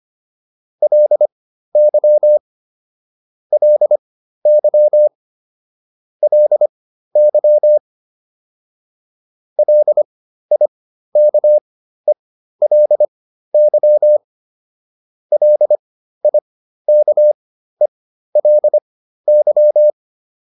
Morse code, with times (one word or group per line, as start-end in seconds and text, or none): 0.8-2.4 LY
3.5-5.1 LY
6.2-7.8 LY
9.6-14.2 LIKELY
15.3-19.9 LIKELY